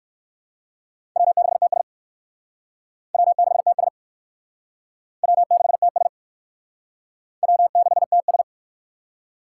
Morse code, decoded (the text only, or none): W6TS W6TS W6TS W6TS